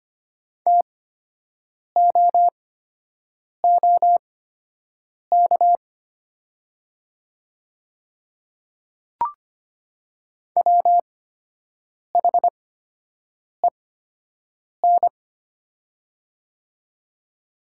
Morse code, heard as a keyed tone, about 25 wpm, spaced as for 7 wpm, 700 Hz. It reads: TOOK WHEN
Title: 8x Character Spacing - Top 300 Words - 25wpm